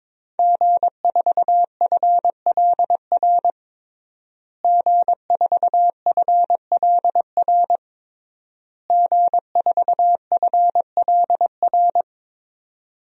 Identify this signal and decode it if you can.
Morse code, heard as a keyed tone, about 22 wpm, 700 Hz. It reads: G4FLR G4FLR G4FLR
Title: DXCC-100 Call Signs - 5 Characters - Repeated 3x - 22wpm